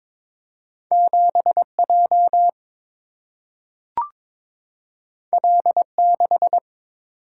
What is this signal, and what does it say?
Morse code, heard as a keyed tone, about 22 wpm, 700 Hz. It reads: 7J L6
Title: Single-Letter Single-Number 22wpm